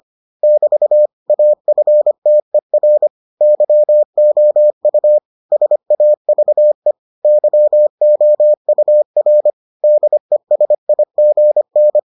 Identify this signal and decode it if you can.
Morse code, heard as a keyed tone, about 25 wpm, 600 Hz. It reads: <BT> AFTER YOU SAVE YOUR DESIGN